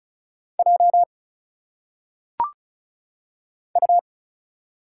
Morse code, heard as a keyed tone, about 36 wpm, 700 Hz. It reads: J U
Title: Single Letters - Phonetic Pronunciation 35wpm